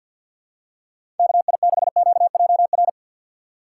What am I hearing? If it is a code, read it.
Morse code, heard as a keyed tone, about 50 wpm, 700 Hz. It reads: KI6QJR
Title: USA Call Signs - 6 Characters - 3x Character Spacing - 50wpm